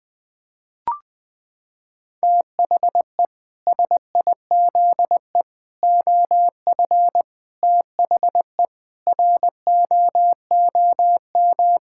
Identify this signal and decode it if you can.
Morse code, heard as a keyed tone, about 20 wpm, 700 Hz. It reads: THE SIZE OF THE ROOM
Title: Sentences with 4-Word Phrases - 20wpm